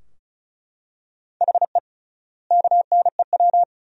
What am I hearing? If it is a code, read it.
Morse code, heard as a keyed tone, about 34 wpm, 700 Hz. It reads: HE KNEW
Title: 3x Word-Spacing - Sentences from Top 500 Words - 35wpm